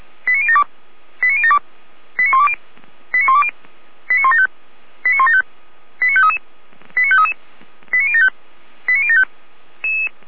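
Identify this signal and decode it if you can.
Five-tone selective call (ZVEI), heard as quick runs of five tones, tones of about 70 ms. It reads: 89072 89072 89120 89120 89175 89175 89533 89533 89075 89075
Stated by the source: ZVEI five-tone call received via SDR.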